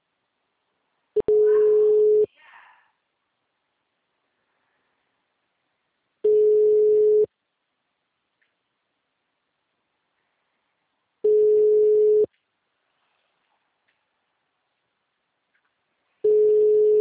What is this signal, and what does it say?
Ringback tone (Europe, 425 Hz).
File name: out- -2002- - - .12299